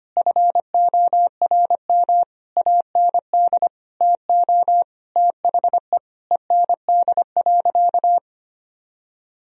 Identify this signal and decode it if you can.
Morse code, heard as a keyed tone, about 25 wpm, 700 Hz. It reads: FORM AND TO THE END.
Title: Sentences from Top 1000 Words - Review Part 1 - 25wpm